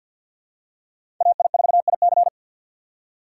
Morse code, heard as a keyed tone, about 52 wpm, 700 Hz. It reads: AI4IC
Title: USA Call Signs - 5 Characters - 2x Character Spacing - 50wpm